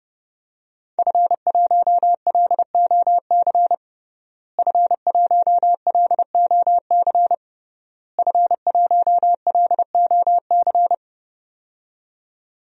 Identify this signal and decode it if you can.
Morse code, heard as a keyed tone, about 31 wpm, 700 Hz. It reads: F1LOC F1LOC F1LOC
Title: DXCC-100 Call Signs - 5 Characters - Repeated 3x - 30wpm